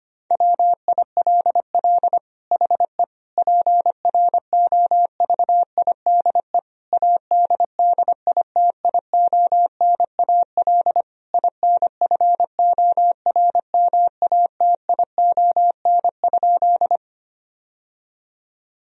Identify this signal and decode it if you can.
Morse code, heard as a keyed tone, about 25 wpm, 700 Hz. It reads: WILL HE PROVIDE ADDITIONAL INFORMATION?
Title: Sentences from Top 1000 Words - Review Part 4 - 25wpm